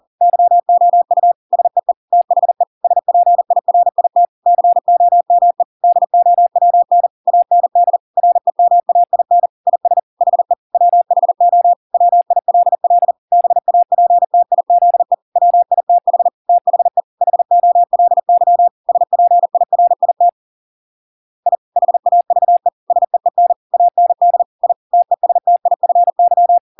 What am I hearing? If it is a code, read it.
Morse code, heard as a keyed tone, about 40 wpm, 700 Hz. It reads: YOU SEE THE SPIRIT COME DOWN AND REMAIN IS HE WHO WILL BAPTIZE WITH THE HOLY SPIRIT I HAVE SEEN AND I TESTIFY